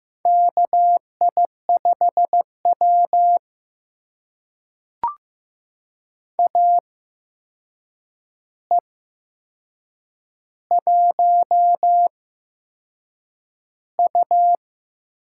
Morse code, heard as a keyed tone, about 15 wpm, spaced as for 6 wpm, 700 Hz. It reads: KI5W AE1U
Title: USA Call Signs - 4 Characters - 8x Character Spacing - 15wpm